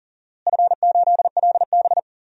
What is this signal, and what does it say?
Morse code, heard as a keyed tone, about 37 wpm, 700 Hz. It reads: F8LB
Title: DXCC-100 Call Signs - 4 Characters - Repeated 2x - 40wpm